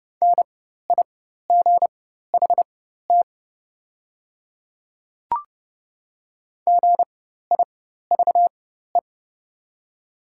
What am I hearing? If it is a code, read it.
Morse code, heard as a keyed tone, about 30 wpm, spaced as for 14 wpm, 700 Hz. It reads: NIGHT GIVE
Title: 4x Character Spacing - Top 300 Words - 30wpm